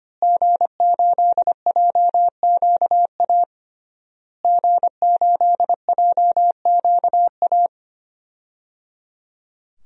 Morse code, heard as a keyed tone, about 25 wpm, 700 Hz. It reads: G8JQA G8JQA
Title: DXCC-100 Call Signs - 5 Characters - Repeated 3x - 25wpm